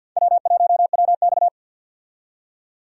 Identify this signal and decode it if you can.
Morse code, heard as a keyed tone, about 52 wpm, 700 Hz. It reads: W1WX
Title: USA Call Signs - 4 Characters - 50wpm